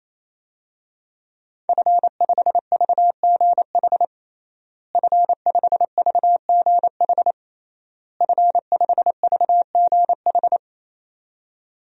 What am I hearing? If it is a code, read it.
Morse code, heard as a keyed tone, about 28 wpm, 700 Hz. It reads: F5VGH F5VGH F5VGH
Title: DXCC-100 Call Signs - 5 Characters - Repeated 3x - 28wpm